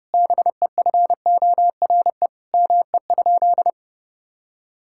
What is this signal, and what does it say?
Morse code, heard as a keyed tone, about 30 wpm, 700 Hz. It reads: BEFORE ME?
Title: Sentences from Top 1000 Words - Review Part 5 - 30wpm